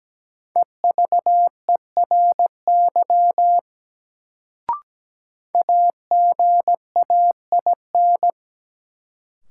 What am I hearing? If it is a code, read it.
Morse code, heard as a keyed tone, about 17 wpm, 700 Hz. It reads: EVERY AGAIN
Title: Sentences from Top 200 Words - New Word Drill - 17wpm